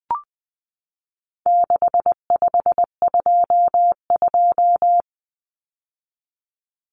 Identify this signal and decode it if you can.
Morse code, heard as a keyed tone, about 20 wpm, 700 Hz. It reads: 6522
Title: Common Cut Numbers - 4-Digit 20wpm